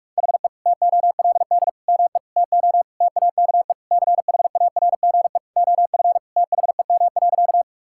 Morse code, heard as a keyed tone, about 46 wpm, 700 Hz. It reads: HE TOLD ME TO TAKE CHARGE OF THEM.
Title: Sentences from Top 1000 Words - Review Part 6 - 45wpm